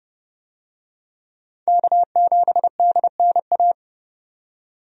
Morse code, heard as a keyed tone, about 31 wpm, 700 Hz. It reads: K7DNA
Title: USA Call Signs - 5 Characters - 30wpm